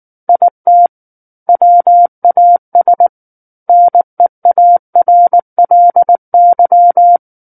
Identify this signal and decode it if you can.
Morse code, heard as a keyed tone, about 19 wpm, 700 Hz. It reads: IT WAS NEARLY
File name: Book One / 10-Chapter 10 - 19 WPM